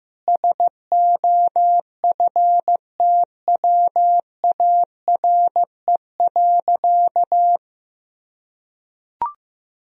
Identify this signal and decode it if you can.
Morse code, heard as a keyed tone, about 15 wpm, 700 Hz. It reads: SOFTWARE.
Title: Sentences from Top 1000 Words - Review Part 3 - 15wpm